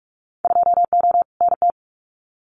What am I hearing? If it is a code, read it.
Morse code, heard as a keyed tone, about 44 wpm, 700 Hz. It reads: 2O NT
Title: Winter Field Day Exchanges Encore 45wpm